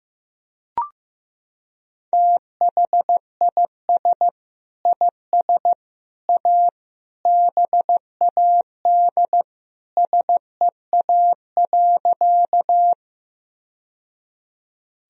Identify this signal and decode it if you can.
Morse code, heard as a keyed tone, about 15 wpm, 700 Hz. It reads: THIS IS A BAD SEA.
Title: Sentences from Top 1000 Words - Review Part 1 - 15wpm